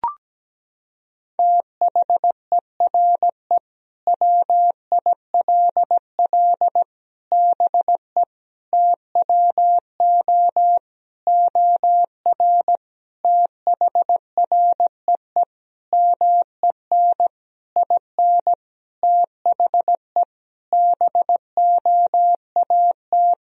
Morse code, heard as a keyed tone, about 17 wpm, 700 Hz. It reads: THERE WILL BE TWO OR THREE MEN IN THE BOAT
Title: Sentences from Top 1000 Words - Review Part 3 - 17wpm